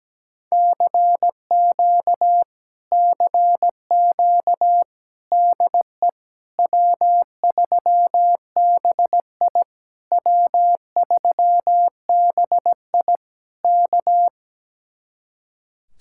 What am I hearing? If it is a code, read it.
Morse code, heard as a keyed tone, about 17 wpm, 700 Hz. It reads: CQ CQ DE W3BI W3BI K
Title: Speed-Racing Common QSO Phrases Encore - 17wpm